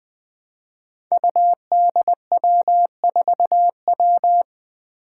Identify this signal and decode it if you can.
Morse code, heard as a keyed tone, about 20 wpm, 700 Hz. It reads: UDW4W
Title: Five Challenging Letter-Number Combinations 20wpm